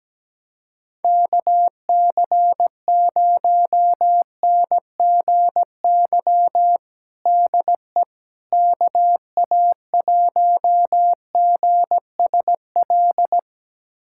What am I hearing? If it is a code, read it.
Morse code, heard as a keyed tone, about 17 wpm, 700 Hz. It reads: KC0NGY DE KA1GSL
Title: USA Callsign DE Callsign - 6 Characters - 17wpm